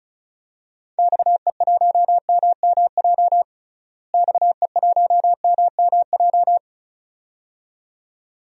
Morse code, heard as a keyed tone, about 36 wpm, 700 Hz. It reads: XE1MMJ XE1MMJ
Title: DXCC-100 Call Signs - 6 Characters - Repeated 2x - 35wpm